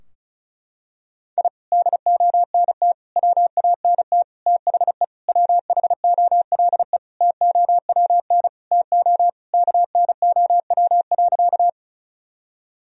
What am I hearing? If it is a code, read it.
Morse code, heard as a keyed tone, about 36 wpm, 700 Hz. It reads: I DONT WANT THE WHOLE TOWN TO KNOW.
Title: Sentences from Top 1000 Words - Review Part 2 - 35wpm